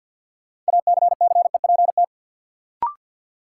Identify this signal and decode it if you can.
Morse code, heard as a keyed tone, about 53 wpm, 700 Hz. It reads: ACCEPT
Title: Sentences from Top 1300 Words - New Word Drill - 50wpm